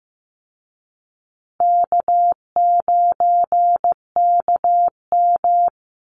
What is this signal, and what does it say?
Morse code, heard as a keyed tone, about 15 wpm, 700 Hz. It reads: K9KM
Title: Call Signs Repeated 3x - 15wpm